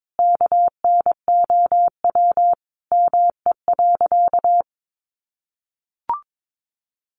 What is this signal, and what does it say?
Morse code, heard as a keyed tone, about 22 wpm, 700 Hz. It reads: KNOW ME.
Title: Sentences from Top 200 Words 22wpm